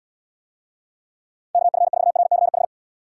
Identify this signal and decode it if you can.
Morse code, heard as a keyed tone, about 78 wpm, 700 Hz. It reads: KF4RCH